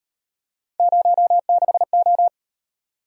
Morse code, heard as a keyed tone, about 38 wpm, 700 Hz. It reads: 06O